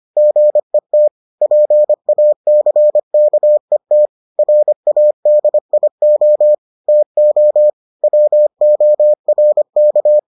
Morse code, heard as a keyed tone, about 25 wpm, 600 Hz. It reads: GET PACKET RADIO TO WORK